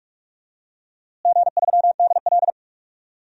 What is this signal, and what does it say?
Morse code, heard as a keyed tone, about 46 wpm, 700 Hz. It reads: G3DL
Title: DXCC-100 Call Signs - 4 Characters - Repeated 3x - 45wpm